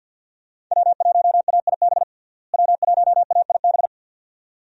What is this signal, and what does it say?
Morse code, heard as a keyed tone, about 52 wpm, 700 Hz. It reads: W1AIB W1AIB
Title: USA Call Signs - 5 Characters - Repeated 2x - 50wpm